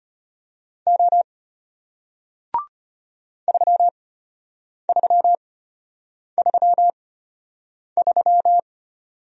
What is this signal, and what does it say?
Morse code, heard as a keyed tone, about 33 wpm, spaced as for 8 wpm, 700 Hz. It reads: O 3333